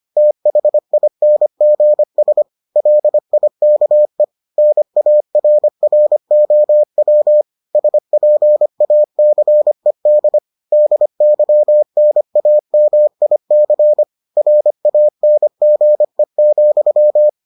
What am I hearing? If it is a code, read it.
Morse code, heard as a keyed tone, about 25 wpm, 600 Hz. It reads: THINGS LIKE NARROW SPACED DYNAMIC RANGE,